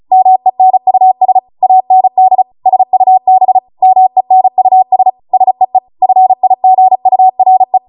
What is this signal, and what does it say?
Morse code, heard as a keyed tone, about 36 wpm, 750 Hz. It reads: MENUS AND SUB MENUS SEE FIGURE